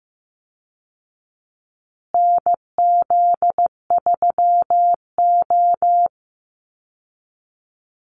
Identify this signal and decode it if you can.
Morse code, heard as a keyed tone, about 15 wpm, 700 Hz. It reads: NZ3O
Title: Call Signs Repeated 2x - 15wpm